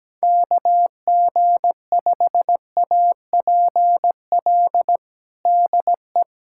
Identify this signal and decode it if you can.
Morse code, heard as a keyed tone, about 17 wpm, 700 Hz. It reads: KG5APL DE
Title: USA Callsign DE Callsign 17wpm